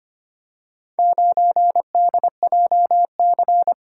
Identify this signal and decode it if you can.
Morse code, heard as a keyed tone, about 25 wpm, 700 Hz. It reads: 9DJC
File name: practice / 25wpm / Four Letter-Number Combinations - 25wpm